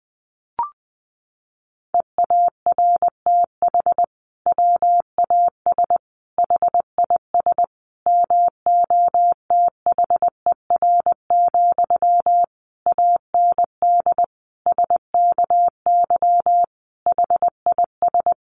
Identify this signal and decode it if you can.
Morse code, heard as a keyed tone, about 20 wpm, 700 Hz. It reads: EARTH WAS HIS MOTHER, AND SKY HIS
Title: Sentences from Top 1200 Words 20wpm